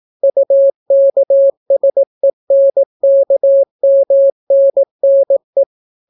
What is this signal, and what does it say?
Morse code, heard as a keyed tone, about 18 wpm, 550 Hz. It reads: UKSENKMNNE